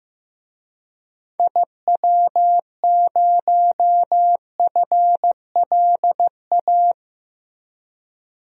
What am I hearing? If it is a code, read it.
Morse code, heard as a keyed tone, about 16 wpm, 700 Hz. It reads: IW0FLA
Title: DXCC-100 Call Signs - 6 Characters - Repeated 3x - 15wpm